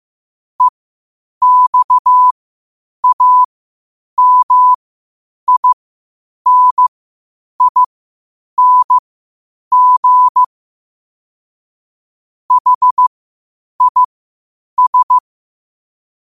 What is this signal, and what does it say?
Morse code, heard as a keyed tone, about 15 wpm, spaced as for 8 wpm, 1000 Hz. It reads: EXAMINING HIS
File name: qotdfiles / 8wpm